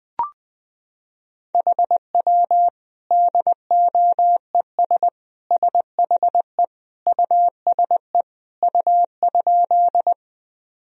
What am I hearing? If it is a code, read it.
Morse code, heard as a keyed tone, about 21 wpm, 700 Hz. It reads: HW DOES SHE USE U?
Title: Sentences from Top 200 Words and Abbreviations 20wpm